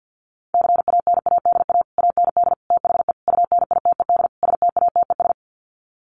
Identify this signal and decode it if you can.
Morse code, heard as a keyed tone, about 50 wpm, 700 Hz. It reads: CANADA AND THE UNITED STATES
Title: Sets of 5 Words Encore 50wpm